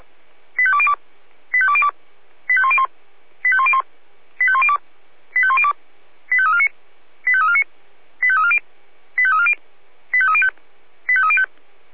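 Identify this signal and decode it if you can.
Five-tone selective call (ZVEI), heard as quick runs of five tones, tones of about 80 ms. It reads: 85202 85202 85101 85101 85102 85102 85349 85349 85340 85340 85305 85305
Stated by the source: ZVEI five-tone call received on an SDR, channel 496.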